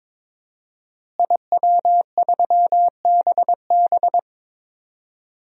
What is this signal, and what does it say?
Morse code, heard as a keyed tone, about 21 wpm, 700 Hz. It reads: IW3BB
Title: DXCC-100 Call Signs - 5 Characters - Repeated 3x - 22wpm